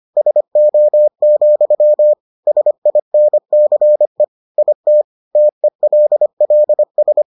Morse code, heard as a keyed tone, about 25 wpm, 600 Hz. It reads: SO, SINCE IT TELLS